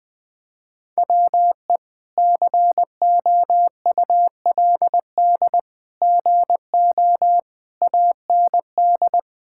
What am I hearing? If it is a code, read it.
Morse code, heard as a keyed tone, about 20 wpm, 700 Hz. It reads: WE COULD GO AND
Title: Sentences from Top 1600 Words 20wpm